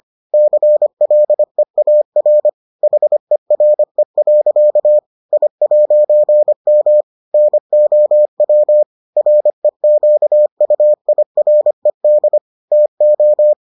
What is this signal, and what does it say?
Morse code, heard as a keyed tone, about 25 wpm, 600 Hz. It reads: CLEAR HERE. I'M NOW REQUIRED TO